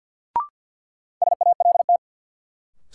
Morse code, heard as a keyed tone, about 52 wpm, 700 Hz. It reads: SALT
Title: Sentences from Top 800 Words - New Word Drill - 50wpm